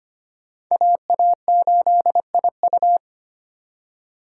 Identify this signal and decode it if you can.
Morse code, heard as a keyed tone, about 25 wpm, 700 Hz. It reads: AA8IU
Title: USA Call Signs - 5 Characters - 2x Character Spacing - 25wpm